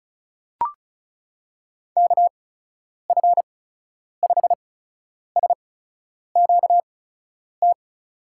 Morse code, heard as a keyed tone, about 36 wpm, spaced as for 10 wpm, 700 Hz. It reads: KF5SQT
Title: USA Call Signs - 8x Character Spacing - 35wpm